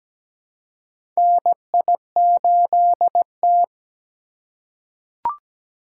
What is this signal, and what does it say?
Morse code, heard as a keyed tone, about 17 wpm, 700 Hz. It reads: NI8T
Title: USA Call Signs - 4 Characters - 3x Character Spacing - 17wpm